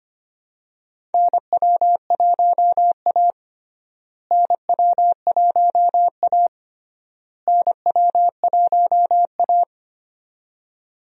Morse code, heard as a keyed tone, about 25 wpm, 700 Hz. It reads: NW1A NW1A NW1A
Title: USA Call Signs - 4 Characters - Repeated 3x - 25wpm